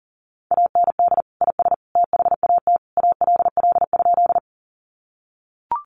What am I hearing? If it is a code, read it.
Morse code, heard as a keyed tone, about 40 wpm, 700 Hz. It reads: AND IS THAT ALL?
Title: Sentences from Top 100 Words 40wpm